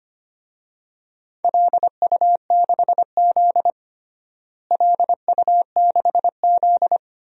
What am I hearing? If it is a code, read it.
Morse code, heard as a keyed tone, about 25 wpm, 700 Hz. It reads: LU6Z LU6Z
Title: DXCC-100 Call Signs - 4 Characters - Repeated 2x - 25wpm